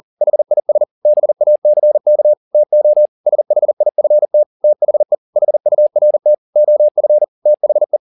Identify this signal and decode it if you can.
Morse code, heard as a keyed tone, about 40 wpm, 600 Hz. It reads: HIS BACK TO SHIFT THE HURT OF THE